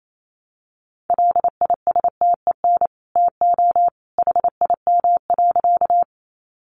Morse code, heard as a keyed tone, about 28 wpm, 700 Hz. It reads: LISTEN TO HIM.
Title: Sentences from Top 400 Words 28wpm